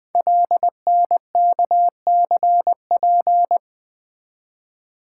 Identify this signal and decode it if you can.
Morse code, heard as a keyed tone, about 20 wpm, 700 Hz. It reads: LNKCP
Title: Code Groups - 5 Letter:Numbers - 20wpm